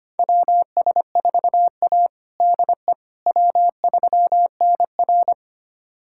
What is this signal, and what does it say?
Morse code, heard as a keyed tone, about 25 wpm, 700 Hz. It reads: WS4A DE W3NR
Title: USA Callsign DE Callsign - 4 Characters - 25wpm